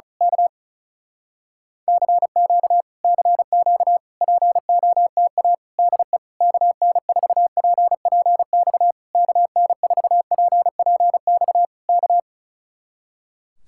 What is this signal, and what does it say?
Morse code, heard as a keyed tone, about 32 wpm, 700 Hz. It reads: K CQ CQ POTA DE KN4PPX KN4PPX K
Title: Speed-Racing Common QSO Phrases Encore - 35wpm